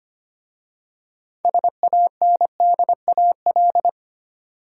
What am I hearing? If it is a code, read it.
Morse code, heard as a keyed tone, about 25 wpm, 700 Hz. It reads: SANDAL
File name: practice / 25wpm / Easy Words with Hints - 25wpm